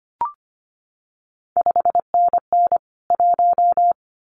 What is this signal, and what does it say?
Morse code, heard as a keyed tone, about 25 wpm, 700 Hz. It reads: 5NN 1
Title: CQ WW DX Contest Exchange Encore - 25wpm